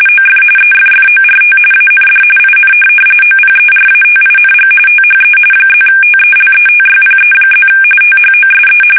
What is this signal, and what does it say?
RTTY, about 45.45 baud. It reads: OF EFFORT AND IN OPERATING THE +TELETYPE+ THOSE